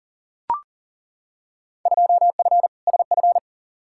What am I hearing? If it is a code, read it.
Morse code, heard as a keyed tone, about 37 wpm, 700 Hz. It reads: 2F SF
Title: Field Day Exchanges - 40wpm